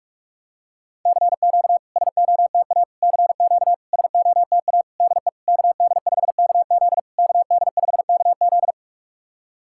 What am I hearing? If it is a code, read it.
Morse code, heard as a keyed tone, about 44 wpm, 690 Hz. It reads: CQ SOTA CQ SOTA DE KD5KZ KD5KZ
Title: SOTA QSO Practice 45wpm